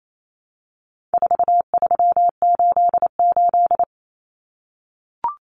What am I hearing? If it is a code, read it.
Morse code, heard as a keyed tone, about 28 wpm, 700 Hz. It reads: 4388
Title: Common Cut Numbers - 4-Digit 28wpm